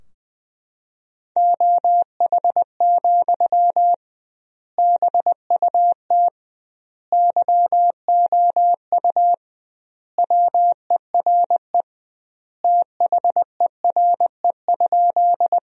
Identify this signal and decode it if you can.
Morse code, heard as a keyed tone, about 20 wpm, 700 Hz. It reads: OH, BUT YOU WERE THERE?
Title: 2x Word-Spacing - Sentences from Top 500 Words - 20wpm